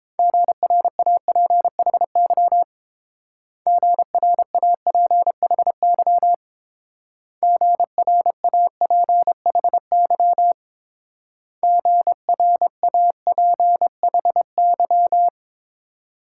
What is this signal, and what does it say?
Morse code, heard as a keyed tone, about 27 wpm, 700 Hz. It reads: GRAPHY GRAPHY GRAPHY GRAPHY
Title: Speed-Racing Multi-letter Phonemes - 22wpm